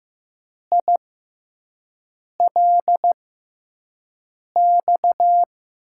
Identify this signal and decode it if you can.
Morse code, heard as a keyed tone, about 15 wpm, spaced as for 5 wpm, 700 Hz. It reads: ILX
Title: USA Call Signs - 6 Characters - 6x Character Spacing - 15wpm